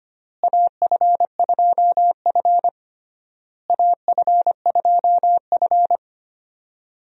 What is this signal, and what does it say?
Morse code, heard as a keyed tone, about 25 wpm, 700 Hz. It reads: AF2F AF2F
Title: USA Call Signs - 4 Characters - Repeated 3x - 25wpm